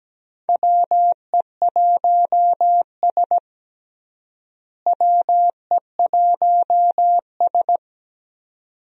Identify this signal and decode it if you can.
Morse code, heard as a keyed tone, about 17 wpm, 700 Hz. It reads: WE1S WE1S
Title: USA Call Signs - 4 Characters - Repeated 2x - 17wpm